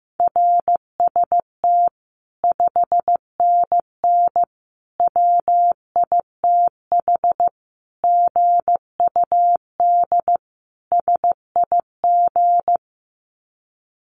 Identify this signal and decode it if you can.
Morse code, heard as a keyed tone, about 15 wpm, 700 Hz. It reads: RST 5NN WITH GUD SIG
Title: Common QSO Phrases 15wpm